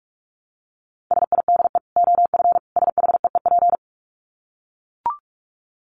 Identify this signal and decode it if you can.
Morse code, heard as a keyed tone, about 44 wpm, 700 Hz. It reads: SIDE OF SHEEP
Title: Dit Challenge 45wpm